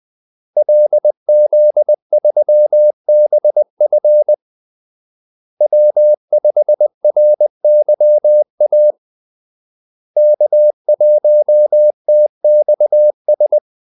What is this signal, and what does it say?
Morse code, heard as a keyed tone, about 20 wpm, 600 Hz. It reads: LZ3BF W5RYA K1TXS
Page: Kaldesignaler 20 wpm